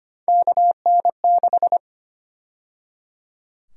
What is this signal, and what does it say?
Morse code, heard as a keyed tone, about 25 wpm, 700 Hz. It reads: KN6
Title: USA Call Sign Prefixes - 25wpm